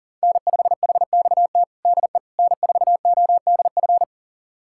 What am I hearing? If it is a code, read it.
Morse code, heard as a keyed tone, about 39 wpm, 700 Hz. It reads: N5HXT DE N4ODF
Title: USA Callsign DE Callsign - 5 Characters - 40wpm